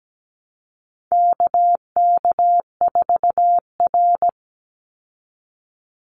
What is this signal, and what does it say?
Morse code, heard as a keyed tone, about 17 wpm, 700 Hz. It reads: KK4R
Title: Call Signs Repeated 2x - 17wpm